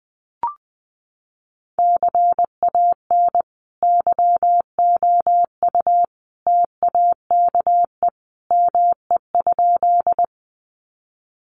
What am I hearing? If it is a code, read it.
Morse code, heard as a keyed tone, about 20 wpm, 700 Hz. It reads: CAN YOU TAKE ME?
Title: Sentences from Top 200 Words 20wpm